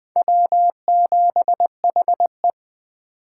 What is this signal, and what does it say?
Morse code, heard as a keyed tone, about 20 wpm, 700 Hz. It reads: W7HE
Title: USA Call Signs - 4 Characters - 8x Character Spacing - 20wpm